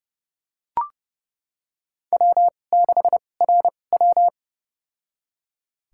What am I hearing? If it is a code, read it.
Morse code, heard as a keyed tone, about 30 wpm, spaced as for 22 wpm, 700 Hz. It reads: W6RW
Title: USA Call Signs - 4 Characters - 2x Character Spacing - 30wpm